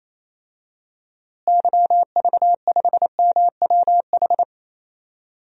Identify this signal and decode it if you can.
Morse code, heard as a keyed tone, about 27 wpm, 700 Hz. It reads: YV5MWH
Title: DXCC-100 Call Signs - 6 Characters - Repeated 2x - 28wpm